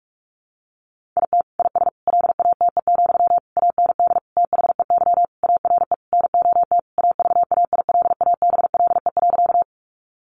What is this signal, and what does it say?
Morse code, heard as a keyed tone, about 44 wpm, 700 Hz. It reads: IT IS LATE, AND THEY ARE NOT AVAILABLE.
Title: Sentences from Top 600 Words 45wpm